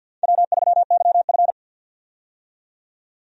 Morse code, heard as a keyed tone, about 50 wpm, 700 Hz. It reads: W3YF
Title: USA Call Signs - 4 Characters - 50wpm